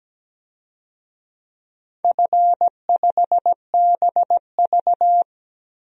Morse code, heard as a keyed tone, about 17 wpm, 700 Hz. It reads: F5BV
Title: DXCC-100 Call Signs - 4 Characters - Repeated 2x - 17wpm